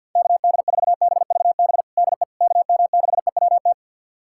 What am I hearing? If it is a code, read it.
Morse code, heard as a keyed tone, about 51 wpm, 700 Hz. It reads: KD4BVB DE KM6EWT
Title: USA Callsign DE Callsign - 6 Characters - 50wpm